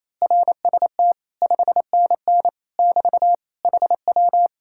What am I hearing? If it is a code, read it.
Morse code, heard as a keyed tone, about 26 wpm, 700 Hz. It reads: RST 5NN <BT> HW
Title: Speed-Racing Common QSO Phrases - 28wpm